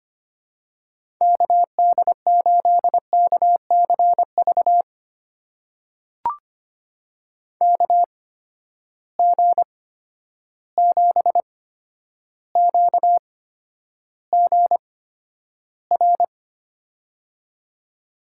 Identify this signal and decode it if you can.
Morse code, heard as a keyed tone, about 25 wpm, spaced as for 11 wpm, 700 Hz. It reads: KD8KCV KG7QGR
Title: USA Call Signs - 6 Characters - 8x Character Spacing - 25wpm